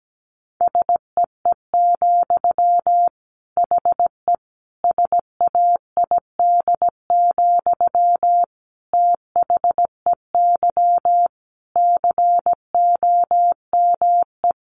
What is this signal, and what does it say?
Morse code, heard as a keyed tone, about 17 wpm, 700 Hz. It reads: SEE, HE SAID, THEY COME
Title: Sentences from Top 700 Words 17wpm